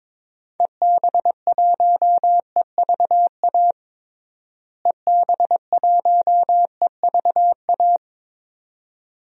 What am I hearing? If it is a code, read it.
Morse code, heard as a keyed tone, about 22 wpm, 700 Hz. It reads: EB1EVA EB1EVA
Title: DXCC-100 Call Signs - 6 Characters - Repeated 2x - 22wpm